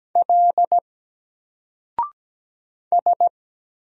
Morse code, heard as a keyed tone, about 17 wpm, 700 Hz. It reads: L S
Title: Single Character (Letter:Number:Punctuation:Prosign) 17wpm